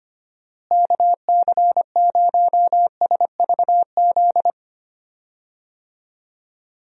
Morse code, heard as a keyed tone, about 25 wpm, 700 Hz. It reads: KC0SVZ
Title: USA Call Signs - 6 Characters - Repeated 3x - 25wpm